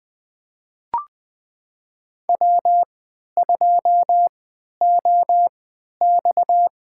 Morse code, heard as a keyed tone, about 20 wpm, spaced as for 11 wpm, 700 Hz. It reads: W2OX
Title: USA Call Signs - 3x Character Spacing - 20wpm